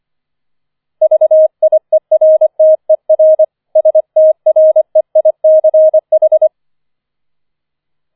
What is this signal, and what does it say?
Morse code, heard as a keyed tone, about 25 wpm, 630 Hz. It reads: VIERTER STREICH